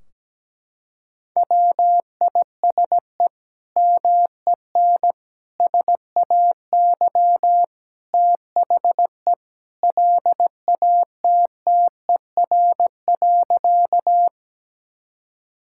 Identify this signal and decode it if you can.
Morse code, heard as a keyed tone, about 17 wpm, 700 Hz. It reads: WISE MEN SAY THE LATTER.
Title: Sentences from Top 1600 Words 17wpm